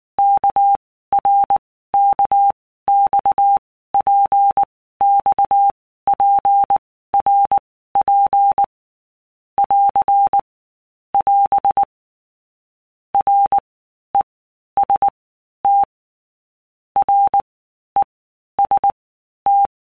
Morse code, heard as a keyed tone, about 19 wpm, 800 Hz. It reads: K R K X P X P R P <AR> <AS> R E S T R E S T
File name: All Groups 1-minute Blocks / Groups@18 / RKPX